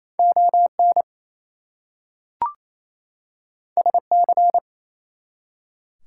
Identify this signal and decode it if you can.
Morse code, heard as a keyed tone, about 28 wpm, 700 Hz. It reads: ON SC
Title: US States and Canadian Provinces 28wpm